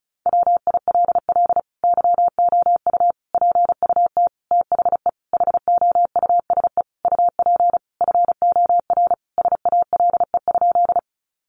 Morse code, heard as a keyed tone, about 34 wpm, 700 Hz. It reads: WILL YOU PUT THE HOUSE UP FOR SALE?
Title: Sentences from Top 1400 Words 35wpm